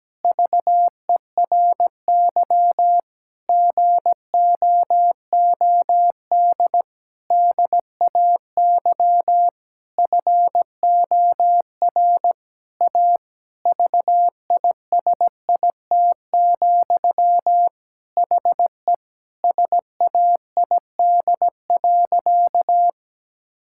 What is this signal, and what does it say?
Morse code, heard as a keyed tone, about 17 wpm, 700 Hz. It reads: VERY GOOD DAY FOR A VISIT, HE SAID.
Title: Sentences from Top 1000 Words - Review Part 6 - 17wpm